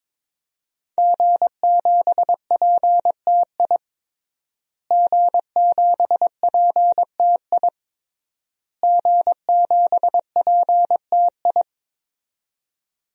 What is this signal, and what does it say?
Morse code, heard as a keyed tone, about 22 wpm, 700 Hz. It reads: G7PTI G7PTI G7PTI
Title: DXCC-100 Call Signs - 5 Characters - Repeated 3x - 22wpm